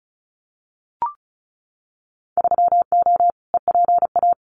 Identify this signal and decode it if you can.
Morse code, heard as a keyed tone, about 36 wpm, 700 Hz. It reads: 3O EPA
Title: Winter Field Day Exchanges 35wpm